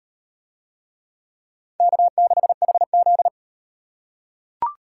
Morse code, heard as a keyed tone, about 35 wpm, 700 Hz. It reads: K6HZ